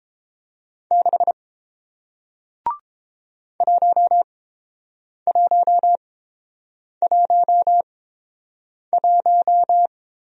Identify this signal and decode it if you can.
Morse code, heard as a keyed tone, about 30 wpm, spaced as for 7 wpm, 700 Hz. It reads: 6 1111